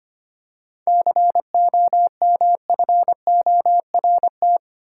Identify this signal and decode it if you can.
Morse code, heard as a keyed tone, about 25 wpm, 700 Hz. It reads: COMFORT
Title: Seven Letter Words - 25wpm